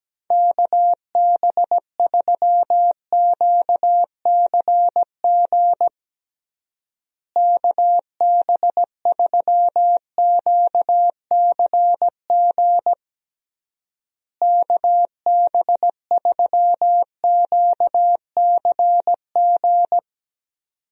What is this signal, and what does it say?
Morse code, heard as a keyed tone, about 17 wpm, 700 Hz. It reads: KB3QCG KB3QCG KB3QCG
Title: USA Call Signs - 6 Characters - Repeated 3x - 17wpm